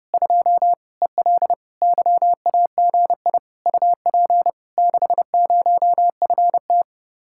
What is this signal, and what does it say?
Morse code, heard as a keyed tone, about 33 wpm, 700 Hz. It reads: 2 EL YAGI UP 60FT